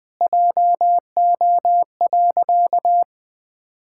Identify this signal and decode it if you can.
Morse code, heard as a keyed tone, about 20 wpm, 700 Hz. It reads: JO.
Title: Pangram Challenge v2 - 20wpm